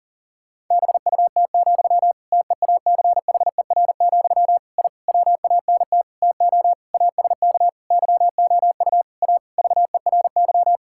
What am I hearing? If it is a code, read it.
Morse code, heard as a keyed tone, about 37 wpm, 700 Hz. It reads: BUT, TEACHER, I WANT TO ASK YOU A VERY
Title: Sentences from Top 1000 Words - Review Part 6 - 40wpm